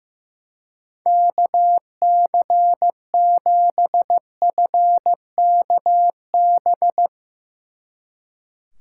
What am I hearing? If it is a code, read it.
Morse code, heard as a keyed tone, about 15 wpm, 700 Hz. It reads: KC7FKB
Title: Speed Racing - USA Call Signs 15wpm